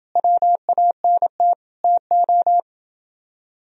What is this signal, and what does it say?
Morse code, heard as a keyed tone, about 28 wpm, 700 Hz. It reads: WANT TO